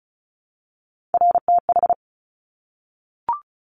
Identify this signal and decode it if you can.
Morse code, heard as a keyed tone, about 36 wpm, 700 Hz. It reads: RTH
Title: Top Three Letter Combinations in Words 35wpm